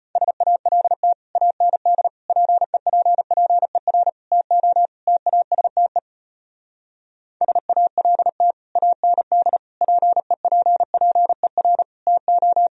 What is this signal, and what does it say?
Morse code, heard as a keyed tone, about 33 wpm, 700 Hz. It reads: SALT AND PEPPER TO TASTE SALT AND PEPPER TO